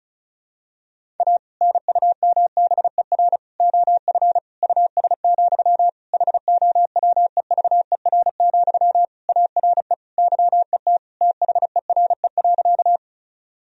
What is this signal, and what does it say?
Morse code, heard as a keyed tone, about 36 wpm, 700 Hz. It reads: A NUMBER OF US, HOWEVER, ARE YET THERE.
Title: Sentences from Top 1000 Words - Review Part 1 - 35wpm